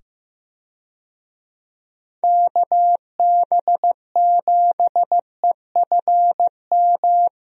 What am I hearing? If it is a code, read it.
Morse code, heard as a keyed tone, about 15 wpm, 700 Hz. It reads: KB7EFM
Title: USA Call Signs - 6 Characters - 15wpm